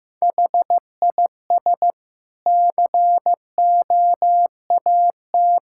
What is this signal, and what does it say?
Morse code, heard as a keyed tone, about 16 wpm, 700 Hz. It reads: HIS COAT
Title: Sentences from Top 1600 Words 15wpm